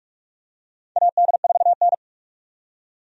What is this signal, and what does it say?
Morse code, heard as a keyed tone, about 45 wpm, 700 Hz. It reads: AD4N
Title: USA Call Signs - 4 Characters - 45wpm